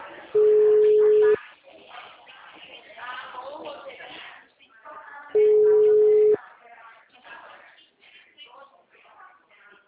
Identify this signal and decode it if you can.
Ringback tone (Europe, 425 Hz), with speech.
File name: out- -2010- - - .55015